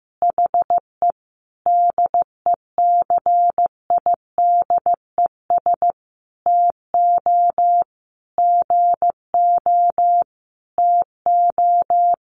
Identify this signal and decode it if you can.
Morse code, heard as a keyed tone, about 15 wpm, 700 Hz. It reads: HE DECIDES TO GO TO
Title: Easy Phrases 15wpm